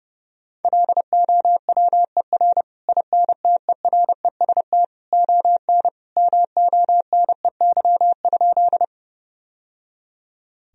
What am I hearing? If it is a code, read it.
Morse code, heard as a keyed tone, about 30 wpm, 700 Hz. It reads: LOWER INTEREST ON MONEY?
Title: Sentences from Top 1000 Words - Review Part 4 - 30wpm